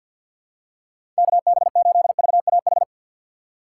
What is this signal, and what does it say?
Morse code, heard as a keyed tone, about 49 wpm, 700 Hz. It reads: KB8VAH
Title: USA Call Signs - 6 Characters - 6x Character Spacing - 50wpm